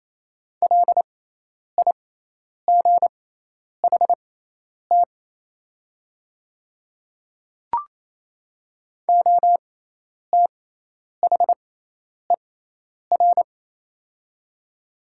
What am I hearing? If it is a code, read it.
Morse code, heard as a keyed tone, about 27 wpm, spaced as for 10 wpm, 700 Hz. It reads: LIGHT OTHER
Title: 6x Character Spacing - Top 200 Words - 28wpm